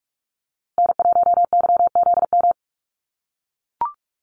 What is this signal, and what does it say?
Morse code, heard as a keyed tone, about 44 wpm, 690 Hz. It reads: N1YZM
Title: USA Call Signs Repeated 3x 45wpm